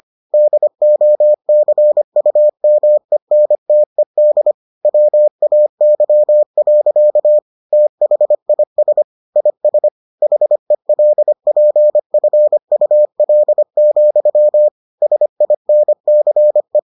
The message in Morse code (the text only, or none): DOCUMENTED WAY. THIS IS HELPFUL, SINCE